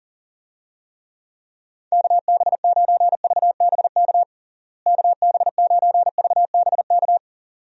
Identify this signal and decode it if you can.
Morse code, heard as a keyed tone, about 40 wpm, 700 Hz. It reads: KB9VBK KB9VBK
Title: USA Call Signs - 6 Characters - Repeated 2x - 40wpm